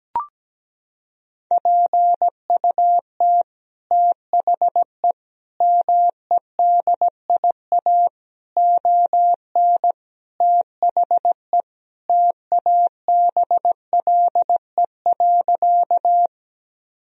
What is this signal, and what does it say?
Morse code, heard as a keyed tone, about 17 wpm, 700 Hz. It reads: PUT THE MEDIA ON THE TABLE.
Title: Sentences from Top 1000 Words - Review Part 4 - 17wpm